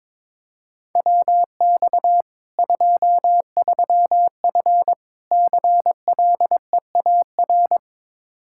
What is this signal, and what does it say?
Morse code, heard as a keyed tone, about 22 wpm, 700 Hz. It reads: WX 23F CLEAR
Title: K1USN SST QSO Practice 22wpm